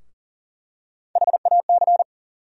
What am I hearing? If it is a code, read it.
Morse code, heard as a keyed tone, about 40 wpm, 700 Hz. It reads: HAC
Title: USA Call Sign Postfixes - 3 Letters 40wpm